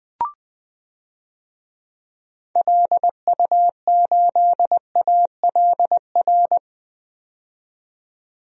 Morse code, heard as a keyed tone, about 20 wpm, 700 Hz. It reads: LU8ALR
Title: DXCC-100 Call Signs - 6 Characters - 20wpm